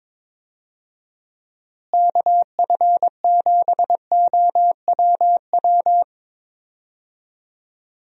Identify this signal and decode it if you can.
Morse code, heard as a keyed tone, about 22 wpm, 700 Hz. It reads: KF7OWW
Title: USA Call Signs - 6 Characters - Repeated 3x - 22wpm